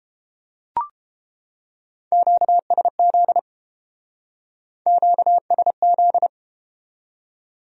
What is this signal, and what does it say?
Morse code, heard as a keyed tone, about 32 wpm, 700 Hz. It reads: QSZ QSZ